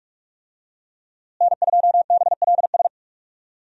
Morse code, heard as a keyed tone, about 43 wpm, 700 Hz. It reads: N2BLS
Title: USA Call Signs - 5 Characters - Repeated 3x - 45wpm